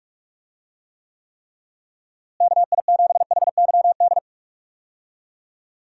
Morse code, heard as a keyed tone, about 44 wpm, 700 Hz. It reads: KI7HYD
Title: USA Call Signs - 6 Characters - Repeated 3x - 45wpm